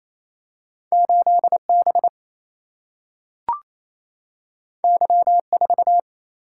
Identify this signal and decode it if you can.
Morse code, heard as a keyed tone, about 28 wpm, 700 Hz. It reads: 8 B Y 4
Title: Single-Letter Single-Number 28wpm